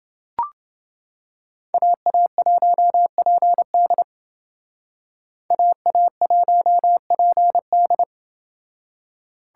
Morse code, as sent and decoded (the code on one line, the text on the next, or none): .- .- .---- .--. -.. / .- .- .---- .--. -..
AA1PD AA1PD